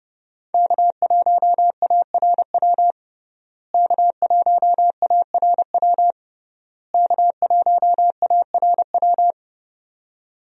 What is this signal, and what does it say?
Morse code, heard as a keyed tone, about 32 wpm, 700 Hz. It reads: K1ARW K1ARW K1ARW
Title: USA Call Signs - 5 Characters - Repeated 3x - 30wpm